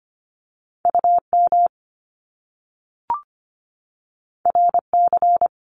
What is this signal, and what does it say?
Morse code, heard as a keyed tone, about 25 wpm, 700 Hz. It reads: UM RC
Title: Top Two Letter Combinations in Words 25wpm